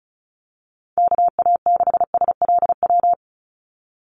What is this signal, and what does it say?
Morse code, heard as a keyed tone, about 34 wpm, 700 Hz. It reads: KA6SLW
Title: USA Call Signs 35wpm